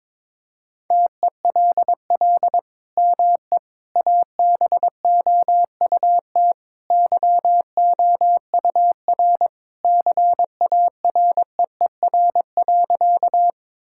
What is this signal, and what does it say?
Morse code, heard as a keyed tone, about 22 wpm, 700 Hz. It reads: TELL ME ABOUT YOUR CAREER.
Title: Sentences from Top 1000 Words - Review Part 6 - 22wpm